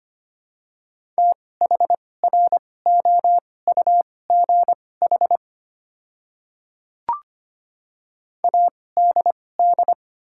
Morse code, heard as a keyed tone, about 25 wpm, spaced as for 18 wpm, 690 Hz. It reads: THROUGH ADD
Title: 2x Character Spacing - Top 500 Words - 25wpm